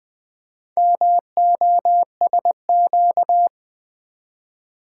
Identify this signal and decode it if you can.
Morse code, heard as a keyed tone, about 20 wpm, 700 Hz. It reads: MOSQ
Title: Build A Word - 2 Letters At A Time - 6-Letter Words - 20wpm